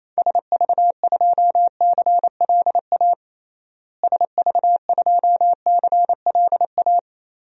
Morse code, heard as a keyed tone, about 28 wpm, 700 Hz. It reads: SV2CLA SV2CLA
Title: DXCC-100 Call Signs - 6 Characters - Repeated 2x - 28wpm